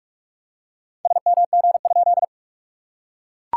Morse code, heard as a keyed tone, about 44 wpm, 700 Hz. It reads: SMG?
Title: SOTA QSO Practice 45wpm